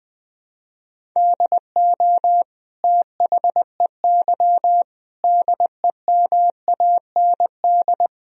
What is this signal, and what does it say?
Morse code, heard as a keyed tone, about 20 wpm, 700 Hz. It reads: DO THEY DEMAND
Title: Sentences from Top 1000 Words - Review Part 3 - 20wpm